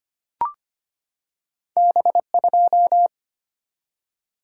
Morse code, heard as a keyed tone, about 25 wpm, 700 Hz. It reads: B2
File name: Two Letter-Number Combinations - 25wpm